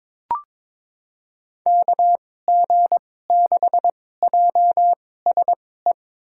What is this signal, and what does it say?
Morse code, heard as a keyed tone, about 22 wpm, spaced as for 16 wpm, 700 Hz. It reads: KG6JSE
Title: USA Call Signs - 6 Characters - 2x Character Spacing - 22wpm